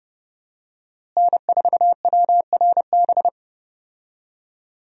Morse code, heard as a keyed tone, about 30 wpm, 700 Hz. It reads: N4WRB
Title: USA Call Signs - 5 Characters - Repeated 3x - 30wpm